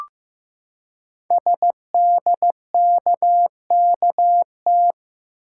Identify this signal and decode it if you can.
Morse code, heard as a keyed tone, about 15 wpm, 700 Hz. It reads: SDKKT
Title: Code Groups - 5 Letters - 15wpm